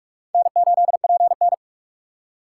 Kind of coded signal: Morse code

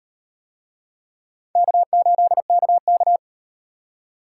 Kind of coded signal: Morse code

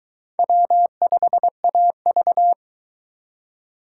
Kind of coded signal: Morse code

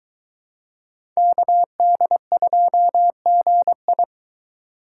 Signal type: Morse code

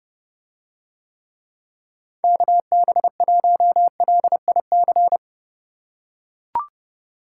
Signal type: Morse code

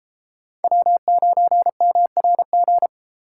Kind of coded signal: Morse code

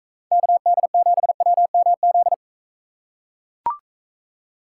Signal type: Morse code

